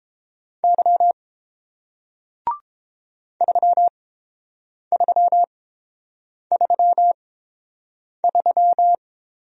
Morse code, heard as a keyed tone, about 30 wpm, spaced as for 7 wpm, 700 Hz. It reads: Y 3333